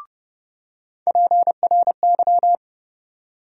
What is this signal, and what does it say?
Morse code, heard as a keyed tone, about 31 wpm, 700 Hz. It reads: PRY
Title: USA Call Sign Postfixes - 3 Letters 30wpm